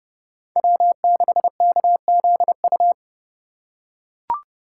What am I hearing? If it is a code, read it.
Morse code, heard as a keyed tone, about 30 wpm, 700 Hz. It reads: W6KZU
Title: USA Call Signs - 4x Character Spacing - 30wpm